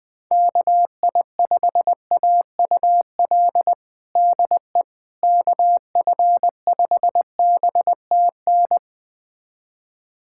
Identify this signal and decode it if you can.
Morse code, heard as a keyed tone, about 20 wpm, 700 Hz. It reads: KI5AUL DE KF5BTN
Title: USA Callsign DE Callsign - 6 Characters - 20wpm